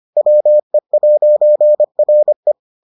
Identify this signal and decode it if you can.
Morse code, heard as a keyed tone, about 25 wpm, 600 Hz. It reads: WE'RE